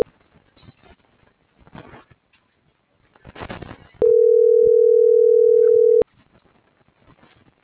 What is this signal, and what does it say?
Ringback tone (North America), 440 + 480 Hz.